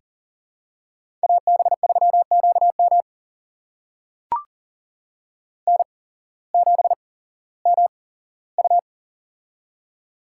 Morse code, heard as a keyed tone, about 40 wpm, spaced as for 19 wpm, 700 Hz. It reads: AB3QM N7MU
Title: USA Call Signs - 8x Character Spacing - 40wpm